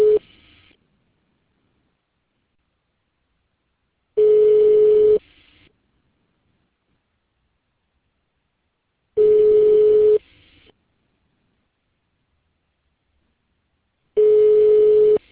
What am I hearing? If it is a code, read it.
Ringback tone (Europe, 425 Hz).